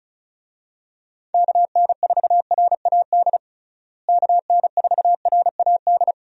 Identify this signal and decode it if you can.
Morse code, heard as a keyed tone, about 36 wpm, 700 Hz. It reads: KN4RAD KN4RAD
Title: USA Call Signs - 6 Characters - Repeated 2x - 35wpm